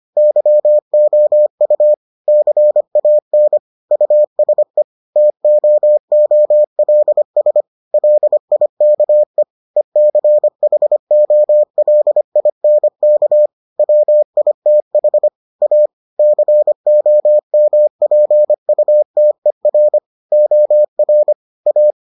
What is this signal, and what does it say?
Morse code, heard as a keyed tone, about 25 wpm, 600 Hz. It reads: YOU CAN USE TOOLS LIKE ECHOLINK WITH A COMPUTER OR A